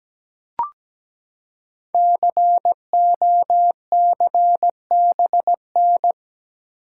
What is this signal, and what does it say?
Morse code, heard as a keyed tone, about 17 wpm, 700 Hz. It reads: COCBN
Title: Code Groups - 5 Letters - 17wpm